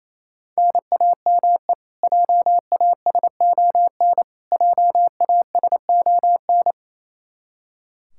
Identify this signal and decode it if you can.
Morse code, heard as a keyed tone, about 27 wpm, 700 Hz. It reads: NAME JASON JASON
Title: K1USN SST QSO Practice 28wpm